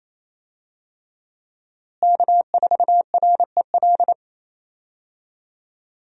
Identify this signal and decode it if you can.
Morse code, heard as a keyed tone, about 28 wpm, 700 Hz. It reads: K4REL
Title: USA Call Signs - 5 Characters - Repeated 3x - 28wpm